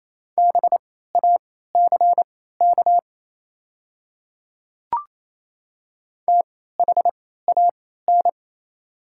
Morse code, heard as a keyed tone, about 28 wpm, spaced as for 16 wpm, 700 Hz. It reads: BACK THAN
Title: 3x Character Spacing - Top 300 Words - 28wpm